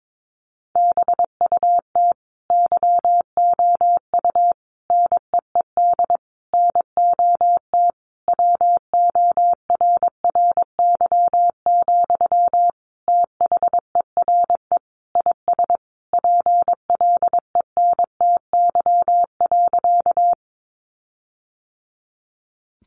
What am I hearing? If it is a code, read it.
Morse code, heard as a keyed tone, about 22 wpm, 700 Hz. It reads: BUT YOU NEED NOT WORRY, THERE IS PLENTY.
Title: Sentences from Top 1300 Words 22wpm